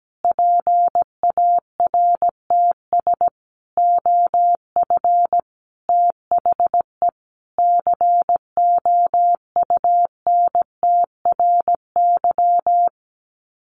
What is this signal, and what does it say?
Morse code, heard as a keyed tone, about 17 wpm, 700 Hz. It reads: PARTS OF THE COUNTRY
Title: Sets of 5 Words Encore 17wpm